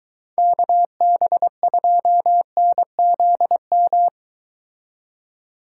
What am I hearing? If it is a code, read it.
Morse code, heard as a keyed tone, about 22 wpm, 700 Hz. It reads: KB2NZM